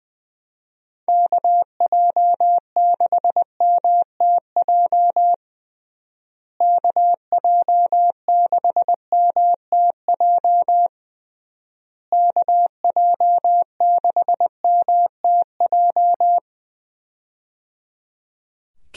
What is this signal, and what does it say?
Morse code, heard as a keyed tone, about 20 wpm, 700 Hz. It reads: KJ6MTJ KJ6MTJ KJ6MTJ
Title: USA Call Signs - 6 Characters - Repeated 3x - 20wpm